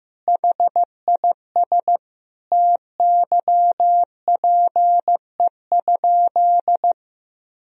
Morse code, heard as a keyed tone, about 15 wpm, 700 Hz. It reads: HIS TYPE?
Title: Sentences from Top 1000 Words - Review Part 3 - 15wpm